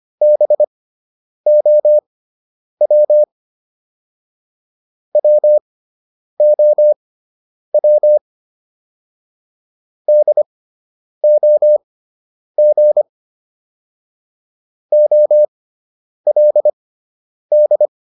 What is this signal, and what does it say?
Morse code, heard as a keyed tone, about 25 wpm, spaced as for 9 wpm, 600 Hz. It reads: BOW WOW DOG OLD